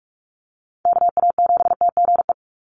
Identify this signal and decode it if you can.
Morse code, heard as a keyed tone, about 46 wpm, 690 Hz. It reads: KA7TGE
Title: USA Call Signs 45wpm